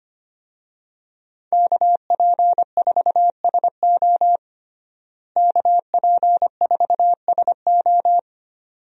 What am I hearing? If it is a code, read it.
Morse code, heard as a keyed tone, about 25 wpm, 700 Hz. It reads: KP4SO KP4SO
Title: DXCC-100 Call Signs - 5 Characters - Repeated 2x - 25wpm